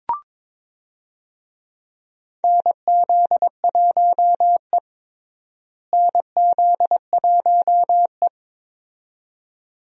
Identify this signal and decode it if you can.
Morse code, heard as a keyed tone, about 22 wpm, 700 Hz. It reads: NZ1E NZ1E
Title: USA Call Signs - 4 Characters - Repeated 2x - 22wpm